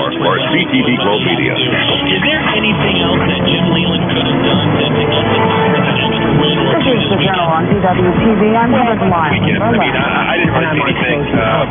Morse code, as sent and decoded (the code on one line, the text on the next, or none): none